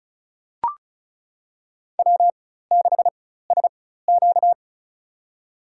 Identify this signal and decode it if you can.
Morse code, heard as a keyed tone, about 36 wpm, spaced as for 16 wpm, 700 Hz. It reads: W6SQ
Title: USA Call Signs - 4 Characters - 4x Character Spacing - 35wpm